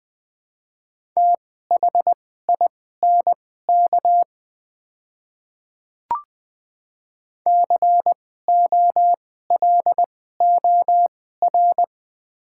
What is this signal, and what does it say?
Morse code, heard as a keyed tone, about 20 wpm, spaced as for 14 wpm, 700 Hz. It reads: THINK COLOR
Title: 2x Character Spacing - Top 300 Words - 20wpm